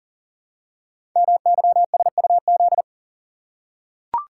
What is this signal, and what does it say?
Morse code, heard as a keyed tone, about 40 wpm, 700 Hz. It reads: MYSUZ
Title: Code Groups - 5 Letters - 40wpm